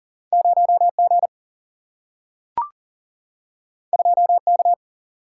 Morse code, heard as a keyed tone, about 39 wpm, 700 Hz. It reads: 0G 2K
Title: Single-Letter Single-Number 40wpm